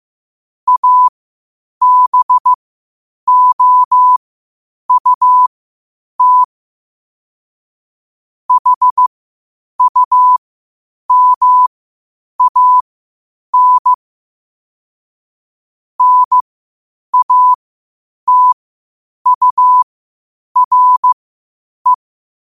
Morse code, heard as a keyed tone, about 15 wpm, spaced as for 8 wpm, 1000 Hz. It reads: ABOUT HUMAN NATURE
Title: 8 WPM morse code quotes for Fri, 01 Mar 2024